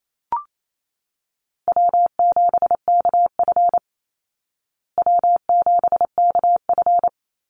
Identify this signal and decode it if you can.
Morse code, heard as a keyed tone, about 28 wpm, 690 Hz. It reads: W7KF W7KF
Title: Call Signs Repeated 3x - 28wpm